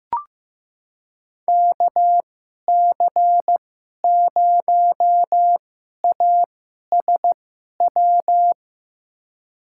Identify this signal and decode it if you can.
Morse code, heard as a keyed tone, about 15 wpm, spaced as for 11 wpm, 700 Hz. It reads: KC0ASW
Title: USA Call Signs - 2x Character Spacing - 15wpm